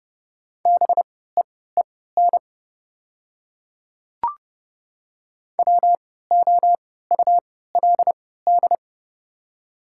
Morse code, heard as a keyed tone, about 30 wpm, spaced as for 17 wpm, 700 Hz. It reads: BEEN WOULD
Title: 3x Character Spacing - Top 200 Words - 30wpm